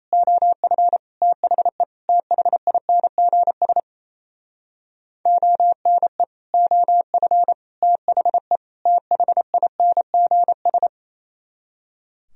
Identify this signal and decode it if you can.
Morse code, heard as a keyed tone, about 36 wpm, 700 Hz. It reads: OF THE THINGS ONE OF THE THINGS